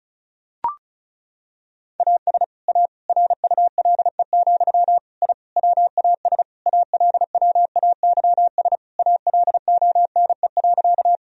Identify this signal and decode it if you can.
Morse code, heard as a keyed tone, about 36 wpm, 700 Hz. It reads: AS A RULE, I WAS ALWAYS ALONE.
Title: Sentences from Top 1000 Words - Review Part 2 - 35wpm